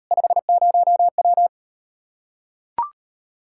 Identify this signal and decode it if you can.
Morse code, heard as a keyed tone, about 38 wpm, 700 Hz. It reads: 50W